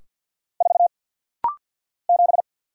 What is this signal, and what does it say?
Morse code, heard as a keyed tone, about 48 wpm, 700 Hz. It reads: 4 7
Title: Single Letter-Number - Rapid-Fire 50wpm